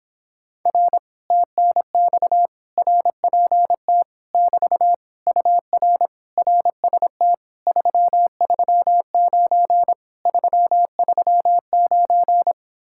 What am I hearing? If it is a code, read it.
Morse code, heard as a keyed tone, about 26 wpm, 700 Hz. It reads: R TNX RPT <BT> UR RST 339 339
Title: Speed-Racing Common QSO Phrases - 22wpm